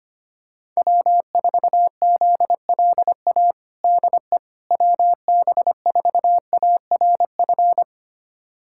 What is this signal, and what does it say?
Morse code, heard as a keyed tone, about 25 wpm, 700 Hz. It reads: W4ZLA DE WB4ARF
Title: USA Callsign DE Callsign 25wpm